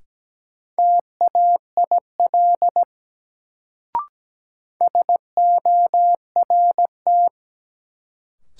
Morse code, heard as a keyed tone, about 17 wpm, 700 Hz. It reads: TAIL SORT
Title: Sentences from Top 1000 Words - Review Part 3 - New Word Drill - 17wpm